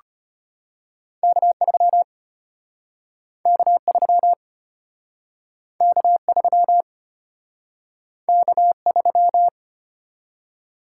Morse code, heard as a keyed tone, about 34 wpm, 700 Hz. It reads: K3 K3 K3 K3